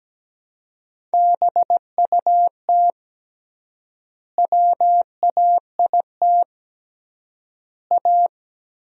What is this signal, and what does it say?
Morse code, heard as a keyed tone, about 17 wpm, 700 Hz. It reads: BUT WAIT A
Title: 3x Word-Spacing - Sentences from Top 500 Words - 17wpm